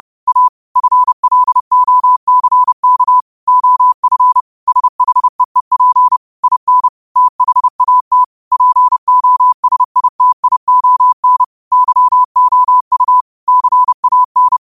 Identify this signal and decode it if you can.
Morse code, heard as a keyed tone, about 30 wpm, 1000 Hz. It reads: A FLOCK OF SHEEP IN THAT POSITION YOU CAN
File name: qotdfiles / 30wpm